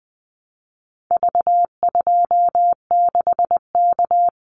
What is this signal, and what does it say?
Morse code, heard as a keyed tone, about 20 wpm, 700 Hz. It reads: V26K
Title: Call Signs Repeated 2x - 20wpm